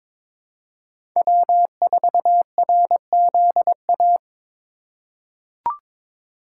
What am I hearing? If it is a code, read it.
Morse code, heard as a keyed tone, about 22 wpm, 700 Hz. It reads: W4RZA
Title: USA Call Signs - 4x Character Spacing - 22wpm